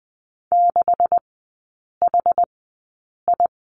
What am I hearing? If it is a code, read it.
Morse code, heard as a keyed tone, about 20 wpm, spaced as for 8 wpm, 700 Hz. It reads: 6HI
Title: 138 Lesson 23e - Letter f - Three Characters 20wpm